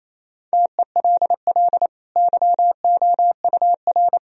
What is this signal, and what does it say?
Morse code, heard as a keyed tone, about 28 wpm, 700 Hz. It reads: TELL YOUR
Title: Sentences from Top 1000 Words - Review Part 2 - 28wpm